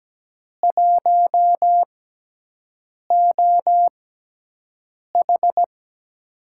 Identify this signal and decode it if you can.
Morse code, heard as a keyed tone, about 17 wpm, spaced as for 6 wpm, 700 Hz. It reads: 1OH
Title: USA Call Signs - 5 Characters - 6x Character Spacing - 17wpm